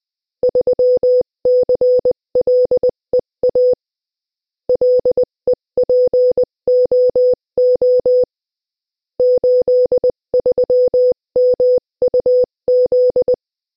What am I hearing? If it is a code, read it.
Morse code, heard as a keyed tone, about 20 wpm, 500 Hz. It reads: 3CLEA LEPOO 83MUZ
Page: Приём на слух сигналов кода Морзе - Тестовое задание 3